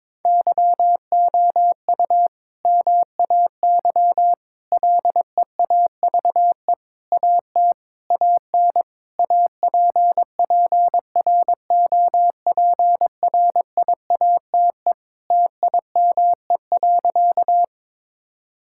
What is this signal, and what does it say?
Morse code, heard as a keyed tone, about 22 wpm, 700 Hz. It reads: YOU MAY LEAVE AT AN APPROPRIATE TIME.
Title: Sentences from Top 1000 Words - Review Part 5 - 22wpm